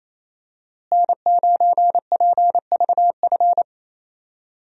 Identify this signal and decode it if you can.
Morse code, heard as a keyed tone, about 28 wpm, 700 Hz. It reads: N9PVF
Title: USA Call Signs - 5 Characters - Repeated 2x - 28wpm